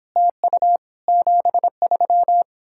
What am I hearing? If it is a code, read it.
Morse code, heard as a keyed tone, about 28 wpm, 700 Hz. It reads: TU 73